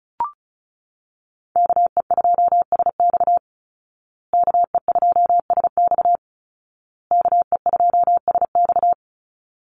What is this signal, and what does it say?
Morse code, heard as a keyed tone, about 34 wpm, 700 Hz. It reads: KE2SX KE2SX KE2SX
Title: Call Signs Repeated 2x - 35wpm